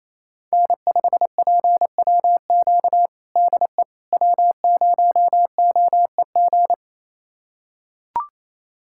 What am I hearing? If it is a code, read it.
Morse code, heard as a keyed tone, about 28 wpm, 700 Hz. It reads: N5PWQ DE W0OEG
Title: USA Callsign DE Callsign - 5 Characters - 28wpm